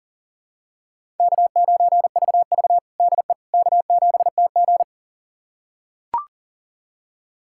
Morse code, heard as a keyed tone, about 37 wpm, 700 Hz. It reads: K9VV DE K7TG
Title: USA Callsign DE Callsign - 4 Characters - 40wpm